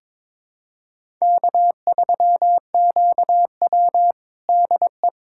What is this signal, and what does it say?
Morse code, heard as a keyed tone, about 22 wpm, 700 Hz. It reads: K3QW DE
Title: USA Callsign DE Callsign - 4 Characters - 22wpm